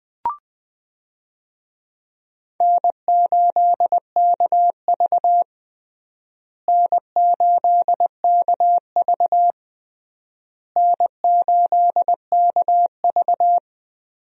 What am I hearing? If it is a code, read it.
Morse code, heard as a keyed tone, about 20 wpm, 700 Hz. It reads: N8KV N8KV N8KV
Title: USA Call Signs - 4 Characters - Repeated 3x - 20wpm